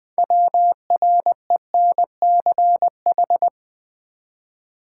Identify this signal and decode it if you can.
Morse code, heard as a keyed tone, about 20 wpm, 700 Hz. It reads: WRENCH
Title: Easy Words with Hints - 20wpm